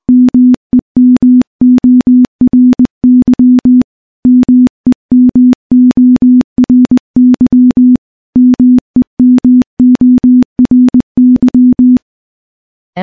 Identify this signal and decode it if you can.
Morse code, heard as a keyed tone, about 20 wpm, 250 Hz. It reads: MEMORY MEMORY MEMORY
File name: morse / morse-training-20WPM